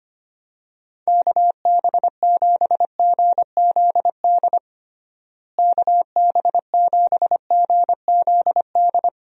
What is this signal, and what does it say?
Morse code, heard as a keyed tone, about 25 wpm, 700 Hz. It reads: KB7GZD KB7GZD
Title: USA Call Signs - 6 Characters - Repeated 2x - 25wpm